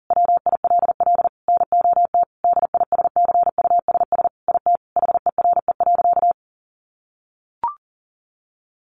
Morse code, heard as a keyed tone, about 40 wpm, 700 Hz. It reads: WILL NOT DISCUSS IT HERE.
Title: Sentences from Top 1100 Words 40wpm